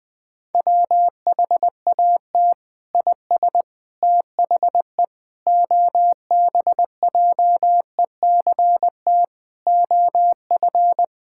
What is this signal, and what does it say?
Morse code, heard as a keyed tone, about 20 wpm, 700 Hz. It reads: WHAT IS THE OBJECT OF
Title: Sentences from Top 1000 Words - Review Part 5 - 20wpm